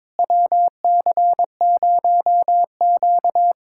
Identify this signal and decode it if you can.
Morse code, heard as a keyed tone, about 22 wpm, 700 Hz. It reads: WC0Q
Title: USA Callsign DE Callsign - 4 Characters - 22wpm